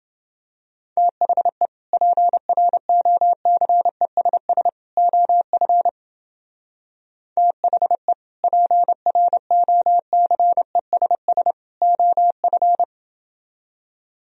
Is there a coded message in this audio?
Yes: Morse code from 1.0 to 12.8 s, heard as a keyed tone, about 30 wpm, 700 Hz. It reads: THE PROCESS OF THE PROCESS OF